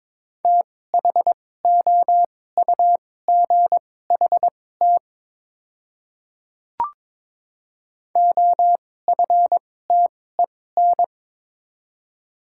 Morse code, heard as a keyed tone, about 22 wpm, spaced as for 16 wpm, 700 Hz. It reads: THOUGHT OFTEN
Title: 2x Character Spacing - Top 300 Words - 22wpm